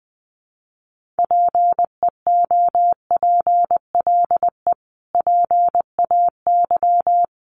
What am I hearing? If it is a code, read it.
Morse code, heard as a keyed tone, about 20 wpm, 700 Hz. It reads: PEOPLE PAY
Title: Sentences from Top 700 Words 20wpm